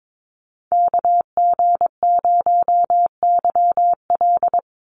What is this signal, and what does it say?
Morse code, heard as a keyed tone, about 22 wpm, 700 Hz. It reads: KG0YL
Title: Call Signs Repeated 2x - 22wpm